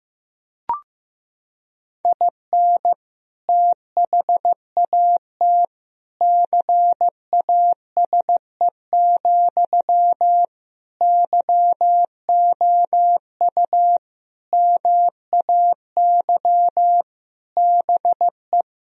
Morse code, heard as a keyed tone, about 15 wpm, 700 Hz. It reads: IN THAT CASE, YOU MAY BE
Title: Sentences from Top 1000 Words - Review Part 1 - 15wpm